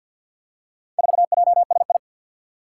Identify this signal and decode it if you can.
Morse code, heard as a keyed tone, about 47 wpm, 700 Hz. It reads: 4JSI
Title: Four Challenging Letter-Number Combinations 50wpm